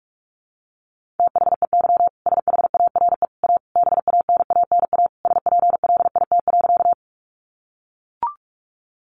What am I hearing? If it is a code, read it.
Morse code, heard as a keyed tone, about 44 wpm, 700 Hz. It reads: THEY SHARE A BANANA SPLIT.
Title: Easy Phrases 45wpm